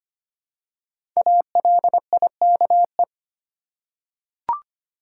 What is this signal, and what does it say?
Morse code, heard as a keyed tone, about 25 wpm, 700 Hz. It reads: ALIKE
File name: 5 Letter Words - First 3 Letters Given - 25wpm